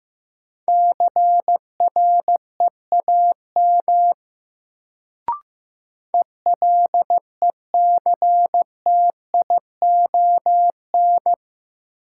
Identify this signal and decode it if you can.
Morse code, heard as a keyed tone, about 15 wpm, 700 Hz. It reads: CREAM ELECTION
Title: Sentences from Top 1500 Words - New Word Drill Alt - 15wpm